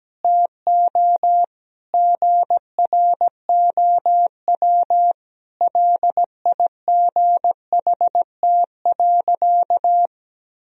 Morse code, heard as a keyed tone, about 17 wpm, 700 Hz. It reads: TO GROW LIGHT.
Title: Sentences from Top 1000 Words - Review Part 1 - 17wpm